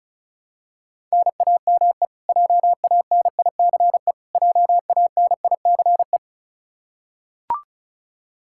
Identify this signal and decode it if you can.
Morse code, heard as a keyed tone, about 34 wpm, 700 Hz. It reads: NAME JANICE JANICE
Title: Practice QSO - Exchange 1 - Beginning - 35wpm